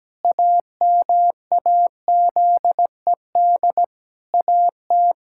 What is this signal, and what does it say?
Morse code, heard as a keyed tone, about 17 wpm, 700 Hz. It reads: AMAZED AT
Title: Pangram Challenge v2 - 17wpm